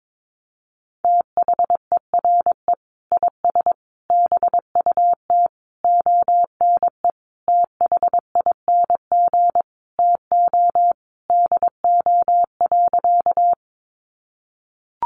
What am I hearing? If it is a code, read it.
Morse code, heard as a keyed tone, about 22 wpm, 700 Hz. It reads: THERE IS BUT ONE THING TO DO.
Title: Sentences from Top 100 Words 22wpm